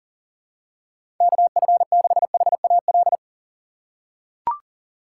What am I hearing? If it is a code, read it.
Morse code, heard as a keyed tone, about 37 wpm, 700 Hz. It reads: KF6HAL
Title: USA Call Signs - 4x Character Spacing - 40wpm